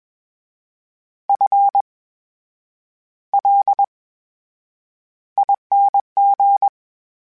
Morse code, heard as a keyed tone, about 21 wpm, spaced as for 8 wpm, 800 Hz. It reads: FLING
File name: Special INGs-mp3s / Special INGs-mp3s / VerySlowToFaster Ing @20